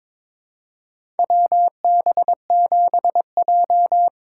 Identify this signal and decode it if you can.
Morse code, heard as a keyed tone, about 22 wpm, 700 Hz. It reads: WB7J
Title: USA Call Signs - 4 Characters - 2x Character Spacing - 22wpm